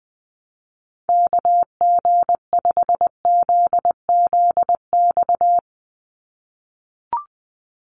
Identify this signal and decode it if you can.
Morse code, heard as a keyed tone, about 20 wpm, 700 Hz. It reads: KG5ZZX
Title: USA Call Signs 20wpm